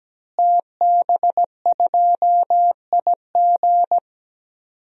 Morse code, heard as a keyed tone, about 17 wpm, 700 Hz. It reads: TB2IG
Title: DXCC-100 Call Signs - 5 Characters - Repeated 3x - 17wpm